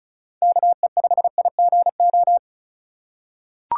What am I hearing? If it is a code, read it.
Morse code, heard as a keyed tone, about 36 wpm, 700 Hz. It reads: KE5IGO
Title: USA Call Signs - 6 Characters - 4x Character Spacing - 35wpm